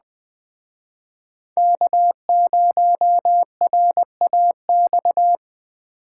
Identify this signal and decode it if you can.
Morse code, heard as a keyed tone, about 20 wpm, 700 Hz. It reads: K0RAX
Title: Speed Racing - USA 1x3 Call Signs 17wpm